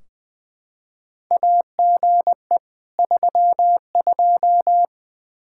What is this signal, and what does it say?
Morse code, heard as a keyed tone, about 21 wpm, 700 Hz. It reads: AGE 32
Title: Practice QSO - Exchange 3 - More Info - 20wpm